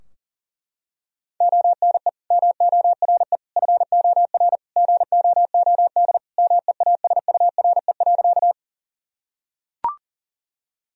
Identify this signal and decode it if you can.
Morse code, heard as a keyed tone, about 40 wpm, 700 Hz. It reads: ONE MORE FOR GOOD MEASURE.
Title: Sentences from Top 1000 Words - Review Part 2 - 40wpm